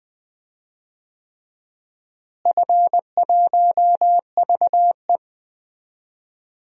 Morse code, heard as a keyed tone, about 20 wpm, 700 Hz. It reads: F1VE
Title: DXCC-100 Call Signs - 4 Characters - Repeated 3x - 20wpm